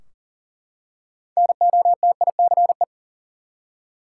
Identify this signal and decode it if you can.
Morse code, heard as a keyed tone, about 40 wpm, 700 Hz. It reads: NOTICE
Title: Six Letter Words - 40wpm